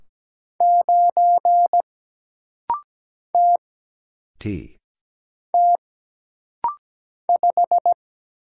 Morse code, heard as a keyed tone, about 17 wpm, 700 Hz. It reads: 9 T T 5
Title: Single Letter-Number - Rapid-Fire 17wpm